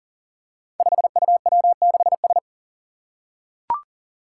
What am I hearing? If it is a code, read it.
Morse code, heard as a keyed tone, about 37 wpm, 700 Hz. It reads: 5UW6S
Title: Five Challenging Letter-Number Combinations 40wpm